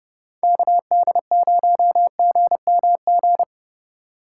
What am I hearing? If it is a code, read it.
Morse code, heard as a keyed tone, about 30 wpm, 700 Hz. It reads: KD0GMG
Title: USA Call Signs - 6 Characters - 4x Character Spacing - 30wpm